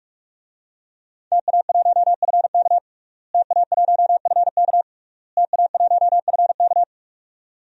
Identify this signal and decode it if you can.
Morse code, heard as a keyed tone, about 46 wpm, 700 Hz. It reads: TA1FK TA1FK TA1FK
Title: DXCC-100 Call Signs - 5 Characters - Repeated 3x - 45wpm